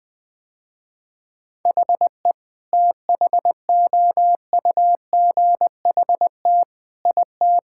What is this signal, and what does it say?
Morse code, heard as a keyed tone, about 20 wpm, 700 Hz. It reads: HE THOUGHT IT
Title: Sentences with 8-Word Phrases - Spoken Beginning - 20wpm